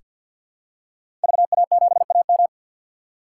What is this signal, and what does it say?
Morse code, heard as a keyed tone, about 49 wpm, 690 Hz. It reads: VA7AM
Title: K1USN SST QSO Practice - 50wpm